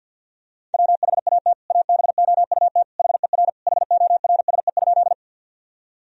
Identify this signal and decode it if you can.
Morse code, heard as a keyed tone, about 50 wpm, 700 Hz. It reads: WHAT ABOUT HER HORSE?
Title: Sentences from Top 1000 Words - Review Part 1 - 50wpm